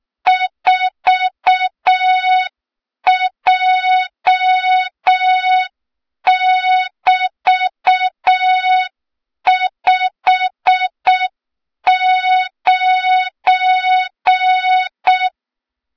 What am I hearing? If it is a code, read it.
Morse code, heard as a keyed tone, about 6 wpm, 3700 Hz. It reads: 4J<BT>59
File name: Morze-HA2MN / hang / 23 GYAKORLAT